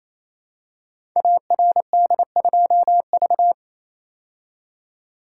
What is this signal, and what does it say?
Morse code, heard as a keyed tone, about 27 wpm, 700 Hz. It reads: ARD2V
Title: Code Groups - 5 Letter:Numbers - 28wpm